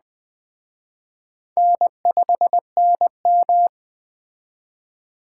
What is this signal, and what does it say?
Morse code, heard as a keyed tone, about 21 wpm, 700 Hz. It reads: N5NM
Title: Speed Racing - USA 1x2 Call Signs 17wpm